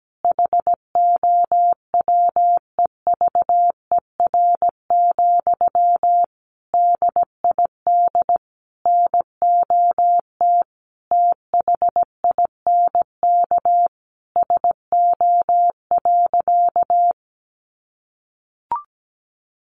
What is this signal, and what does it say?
Morse code, heard as a keyed tone, about 17 wpm, 700 Hz. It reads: HOWEVER, DID NOT THINK SO.
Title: Sentences from Top 1500 Words 17wpm